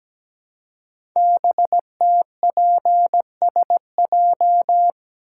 Morse code, heard as a keyed tone, about 17 wpm, 700 Hz. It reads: BTPSJ
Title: Code Groups - 5 Letters - 17wpm